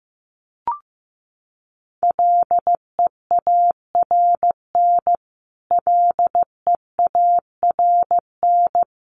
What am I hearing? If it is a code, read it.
Morse code, heard as a keyed tone, about 15 wpm, 700 Hz. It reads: LEARN LEARN
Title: Top 300 Words Repeated 15wpm